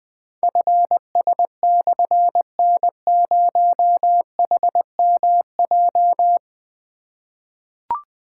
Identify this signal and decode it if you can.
Morse code, heard as a keyed tone, about 20 wpm, 700 Hz. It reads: FS/N0HMJ
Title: Difficult Call Signs 20wpm